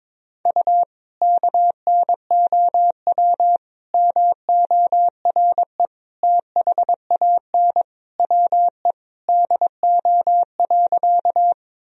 Morse code, heard as a keyed tone, about 22 wpm, 700 Hz. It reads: U KNOW MORE THAN WE DO.
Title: Sentences from Top 100 Words and Abbreviations 22wpm